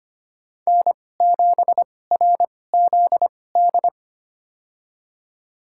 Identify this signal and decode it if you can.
Morse code, heard as a keyed tone, about 25 wpm, spaced as for 18 wpm, 700 Hz. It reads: N7RZD
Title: USA Call Signs - 5 Characters - 2x Character Spacing - 25wpm